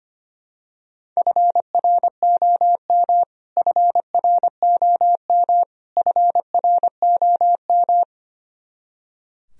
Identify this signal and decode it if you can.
Morse code, heard as a keyed tone, about 25 wpm, 700 Hz. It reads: FROM FROM FROM
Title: Top 100 Words Repeated 25wpm